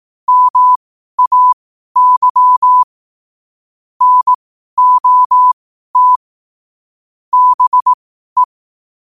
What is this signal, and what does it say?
Morse code, heard as a keyed tone, about 18 wpm, spaced as for 12 wpm, 1000 Hz. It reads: MAY NOT BE
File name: qotdfiles / 12wpm